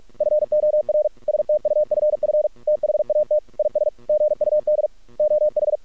Morse code, heard as a keyed tone, about 46 wpm, 600 Hz. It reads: FOR STUFF THAT IS GUD OH
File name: archived / CRSnet